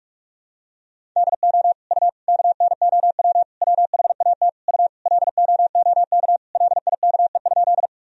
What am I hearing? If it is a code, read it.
Morse code, heard as a keyed tone, about 42 wpm, 700 Hz. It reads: DO U KNOW WHAT U LOOK LIKE?
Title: Sentences from Top 100 Words and Abbreviations 45wpm